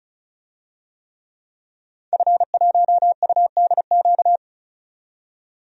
Morse code, heard as a keyed tone, about 34 wpm, 700 Hz. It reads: F1UDQ
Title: DXCC-100 Call Signs - 5 Characters - 35wpm